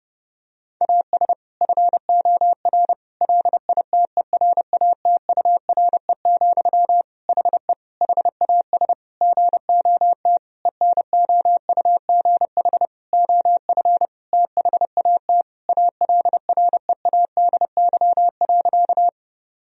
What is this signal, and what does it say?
Morse code, heard as a keyed tone, about 32 wpm, 700 Hz. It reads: AS FOR LITERATURE, HE HAS GOT ENOUGH OF THAT ALREADY.
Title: Sentences from Top 1000 Words - Review Part 3 - 30wpm